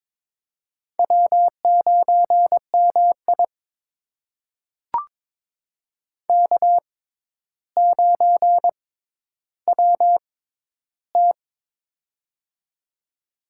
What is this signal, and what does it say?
Morse code, heard as a keyed tone, about 22 wpm, spaced as for 11 wpm, 700 Hz. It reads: W9MI K9WT
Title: USA Call Signs - 4 Characters - 6x Character Spacing - 22wpm